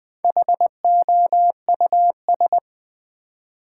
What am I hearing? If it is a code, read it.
Morse code, heard as a keyed tone, about 20 wpm, 700 Hz. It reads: HOUS
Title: Build A Word - 1 Letter At A Time - 5-Letter Words - 20wpm